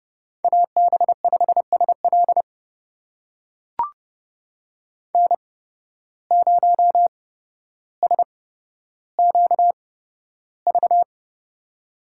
Morse code, heard as a keyed tone, about 30 wpm, spaced as for 13 wpm, 700 Hz. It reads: AB5SL N0SQV
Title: USA Call Signs - 5 Characters - 8x Character Spacing - 30wpm